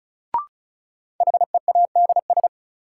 Morse code, heard as a keyed tone, about 36 wpm, 700 Hz. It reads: HEADS
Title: Sentences from Top 1000 Words - Review Part 3 - New Word Drill - 35wpm